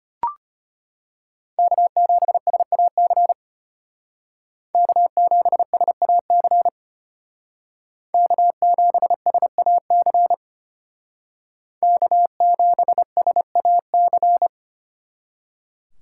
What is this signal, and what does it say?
Morse code, heard as a keyed tone, about 31 wpm, 700 Hz. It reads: K7SAC K7SAC K7SAC K7SAC
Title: Speed Racing - USA 1x3 Call Signs 25wpm